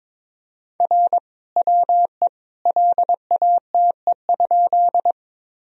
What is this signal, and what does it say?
Morse code, heard as a keyed tone, about 22 wpm, 700 Hz. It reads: R WE LATE?
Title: Sentences from Top 300 Words and Abbreviations 22wpm